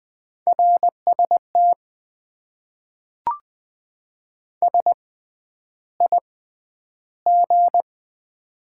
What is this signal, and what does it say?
Morse code, heard as a keyed tone, about 20 wpm, spaced as for 10 wpm, 700 Hz. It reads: RST SIG
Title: Top 90 QSO Elements - 6x Character Spacing - 20wpm